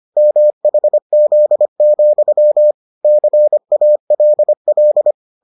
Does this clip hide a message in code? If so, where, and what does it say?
Yes: Morse code from 0.2 to 5.1 s, heard as a keyed tone, about 25 wpm, 600 Hz. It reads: MHZ, CALL